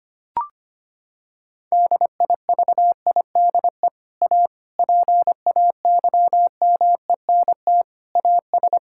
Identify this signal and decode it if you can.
Morse code, heard as a keyed tone, about 25 wpm, 700 Hz. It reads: DIVIDE A PAYMENT AS
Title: Sentences from Top 1600 Words 25wpm